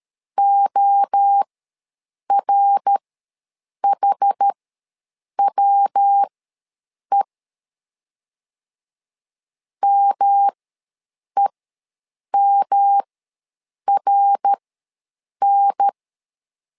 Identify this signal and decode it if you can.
Morse code, heard as a keyed tone, about 13 wpm, spaced as for 7 wpm, 800 Hz. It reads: ORHWE MEMRN
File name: LEZIONI / 3 MWRO / 3 MWRO ws 12wpm 1914